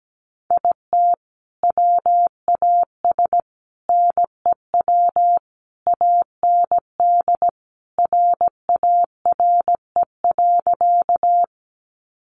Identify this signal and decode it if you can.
Morse code, heard as a keyed tone, about 17 wpm, 700 Hz. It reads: IT WAS NEW AND RARE.
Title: Sentences from Top 1100 Words 17wpm